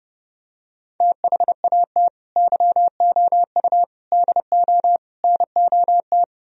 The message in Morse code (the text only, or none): THAT YOU DO NOT